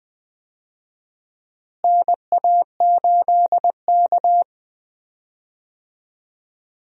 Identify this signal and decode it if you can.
Morse code, heard as a keyed tone, about 21 wpm, 700 Hz. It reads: NA8K
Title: USA Call Signs - 4 Characters - Repeated 3x - 20wpm